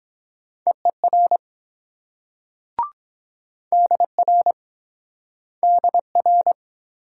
Morse code, heard as a keyed tone, about 22 wpm, 700 Hz. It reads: EER DR DR